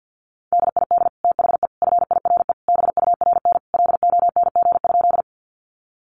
Morse code, heard as a keyed tone, about 50 wpm, 700 Hz. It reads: DID THE FIRE BURN LONG?
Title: Sentences from Top 1300 Words 50wpm